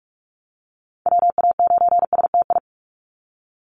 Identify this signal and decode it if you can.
Morse code, heard as a keyed tone, about 44 wpm, 700 Hz. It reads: WA9STI
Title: NA SOTA Activator Callsigns 45wpm